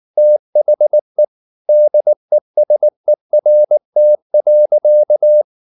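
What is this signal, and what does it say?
Morse code, heard as a keyed tone, about 19 wpm, 600 Hz. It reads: THE DESERT.
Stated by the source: Morse code at 19 wpm from CW med Gnister, reading Ken Follett's novel Never in English.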